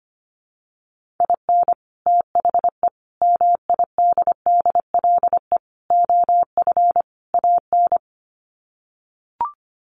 Morse code, heard as a keyed tone, about 25 wpm, 700 Hz. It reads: IN THE MIDDLE OF AN
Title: Sets of 5 Words Encore2 25wpm